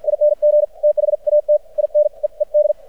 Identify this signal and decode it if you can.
Morse code, heard as a keyed tone, about 53 wpm, spaced as for 43 wpm, 600 Hz. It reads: ING THAT I NEED
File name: archived / CRSnet